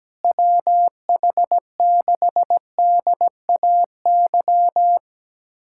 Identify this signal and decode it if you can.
Morse code, heard as a keyed tone, about 17 wpm, 700 Hz. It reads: WH6DAY
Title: USA Callsign DE Callsign - 6 Characters - 17wpm